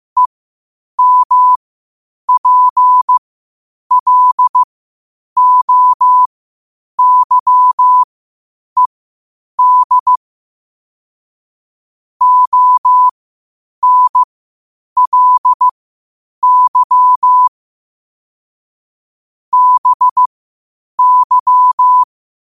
Morse code, heard as a keyed tone, about 15 wpm, spaced as for 8 wpm, 1000 Hz. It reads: EMPLOYED ONLY BY
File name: 8wpm